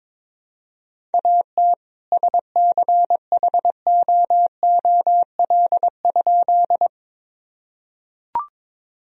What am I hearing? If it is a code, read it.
Morse code, heard as a keyed tone, about 22 wpm, 700 Hz. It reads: AT SCHOOL?
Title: Sentences from Top 1000 Words - Review Part 1 - 22wpm